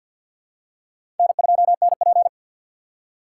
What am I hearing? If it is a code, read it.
Morse code, heard as a keyed tone, about 51 wpm, 700 Hz. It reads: N2NP
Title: USA Call Signs - 4x Character Spacing - 50wpm